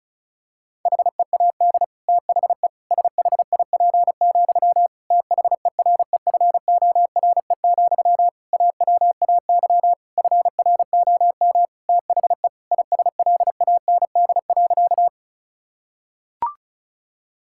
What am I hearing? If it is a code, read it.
Morse code, heard as a keyed tone, about 34 wpm, 700 Hz. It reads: HEAD THE SHIP, THEREFORE, AWAY FROM THE ISLAND.
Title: Sentences from Top 1000 Words - Review Part 5 - 35wpm